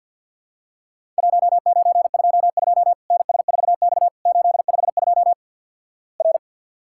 Morse code, heard as a keyed tone, about 50 wpm, 700 Hz. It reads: 1922 NS4X 852 R
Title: Worked All Europe DX Contest - QTC Exchange - 50wpm